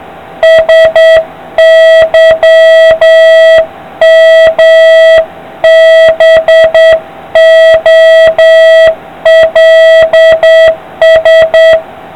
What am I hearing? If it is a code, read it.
Morse code, heard as a keyed tone, about 8 wpm, 650 Hz, with noise. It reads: SYMBOLS